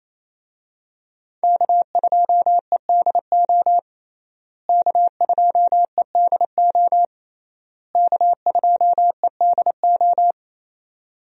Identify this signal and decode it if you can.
Morse code, heard as a keyed tone, about 27 wpm, 700 Hz. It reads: K2EDO K2EDO K2EDO
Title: USA Call Signs - 5 Characters - Repeated 3x - 28wpm